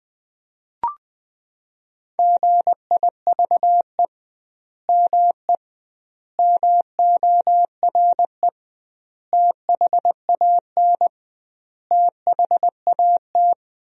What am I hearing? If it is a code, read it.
Morse code, heard as a keyed tone, about 20 wpm, 700 Hz. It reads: GIVE ME MORE THAN THAT
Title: 2x Word-Spacing - Sentences from Top 200 Words - 20wpm